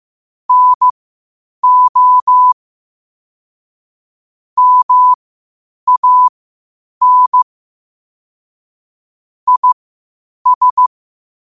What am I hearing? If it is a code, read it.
Morse code, heard as a keyed tone, about 15 wpm, spaced as for 8 wpm, 1000 Hz. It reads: NO MAN IS
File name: qotdfiles / 8wpm